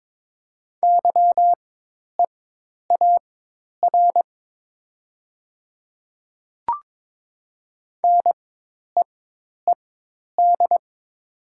Morse code, heard as a keyed tone, about 22 wpm, spaced as for 10 wpm, 700 Hz. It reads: YEAR NEED
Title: 4x Character Spacing - Top 300 Words - 22wpm